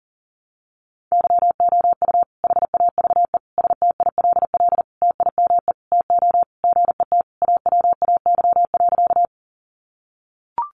Morse code, heard as a keyed tone, about 40 wpm, 700 Hz. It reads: YOU HAVE STILL TIME TO GET AWAY.
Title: Sentences from Top 600 Words 40wpm